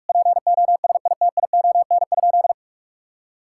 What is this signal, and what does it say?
Morse code, heard as a keyed tone, about 41 wpm, 700 Hz. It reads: POSITION?
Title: Sentences from Top 1000 Words - Review Part 5 - 45wpm